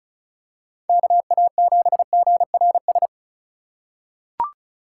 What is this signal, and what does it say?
Morse code, heard as a keyed tone, about 36 wpm, 700 Hz. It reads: KA7GRS
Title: USA Call Signs - 6 Characters - 3x Character Spacing - 35wpm